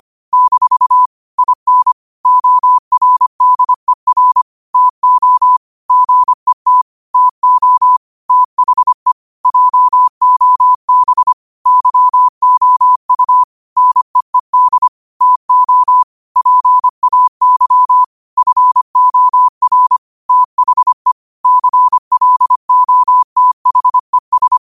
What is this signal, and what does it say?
Morse code, heard as a keyed tone, about 25 wpm, 1000 Hz. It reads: - IN ORDER TO GET TO THE JOB YOU NEED TO PAY FOR THE CLOTHES